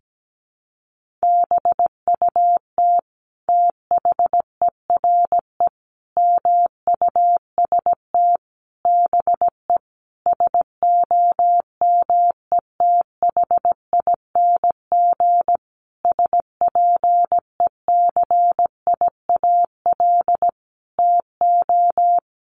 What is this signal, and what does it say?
Morse code, heard as a keyed tone, about 17 wpm, 700 Hz. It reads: BUT THERE MUST BE SOMETHING SPECIAL TO
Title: Sentences from Top 1500 Words 17wpm